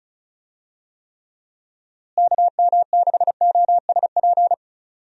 Morse code, heard as a keyed tone, about 34 wpm, 700 Hz. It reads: KM6OSP
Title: USA Call Signs - 6 Characters - Repeated 2x - 35wpm